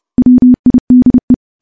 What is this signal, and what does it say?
Morse code, heard as a keyed tone, about 31 wpm, 250 Hz. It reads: WIDE
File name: morse / morse-training-30WPM